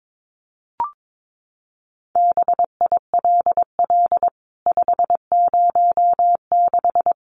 Morse code, heard as a keyed tone, about 22 wpm, 700 Hz. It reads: BILL 506
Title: CWT Exchange 22wpm